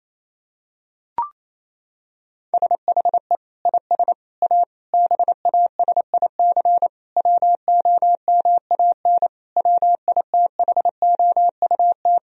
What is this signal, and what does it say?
Morse code, heard as a keyed tone, about 27 wpm, 700 Hz. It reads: SHE IS A BASIC WOMAN WITHOUT
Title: Sentences from Top 1000 Words - Review Part 3 - 28wpm